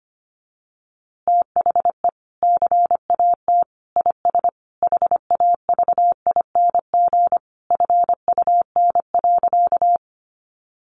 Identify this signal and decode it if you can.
Morse code, heard as a keyed tone, about 25 wpm, 700 Hz. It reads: THE CAT IS HAVING FUN.
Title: Easy Phrases 25wpm